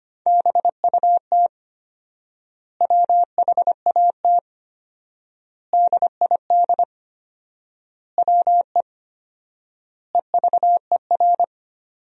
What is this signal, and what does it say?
Morse code, heard as a keyed tone, about 25 wpm, 700 Hz. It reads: BUT WHAT DID WE EVER
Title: 4x Word-Spacing - Sentences from Top 400 Words - 25wpm